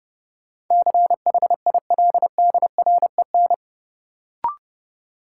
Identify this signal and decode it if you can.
Morse code, heard as a keyed tone, about 30 wpm, 700 Hz. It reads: CHILDREN
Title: Sentences from Top 300 Words - New Word Drill - 30wpm